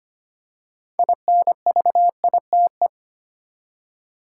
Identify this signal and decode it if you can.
Morse code, heard as a keyed tone, about 25 wpm, 700 Hz. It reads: INVITE
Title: Six Letter Words - 25wpm